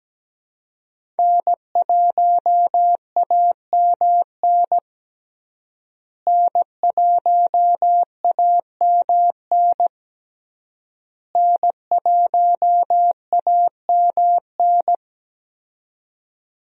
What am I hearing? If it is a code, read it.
Morse code, heard as a keyed tone, about 17 wpm, 700 Hz. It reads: N1AMN N1AMN N1AMN
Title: USA Call Signs - 5 Characters - Repeated 3x - 17wpm